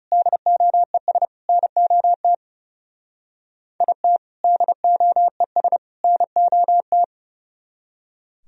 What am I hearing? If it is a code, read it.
Morse code, heard as a keyed tone, about 36 wpm, 700 Hz. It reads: DOES NOT IT DOES NOT
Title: Speed-Racing - Sets of 3 Words 30wpm